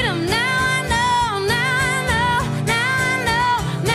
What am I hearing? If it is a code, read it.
Emergency Alert System (SAME) tones, heard as harsh buzzing bursts. It is heard from 0.4 to 3.2 s.